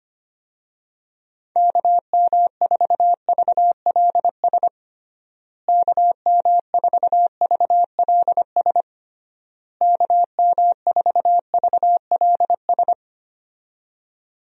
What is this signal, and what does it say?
Morse code, heard as a keyed tone, about 25 wpm, 700 Hz. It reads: KM4VLS KM4VLS KM4VLS
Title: USA Call Signs - 6 Characters - Repeated 3x - 25wpm